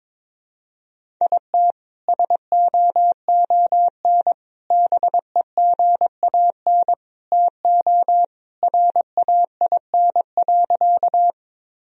Morse code, heard as a keyed tone, about 21 wpm, 700 Hz. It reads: IT SOON BEGAN TO RAIN.
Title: Sentences from Top 1000 Words - Review Part 1 - 22wpm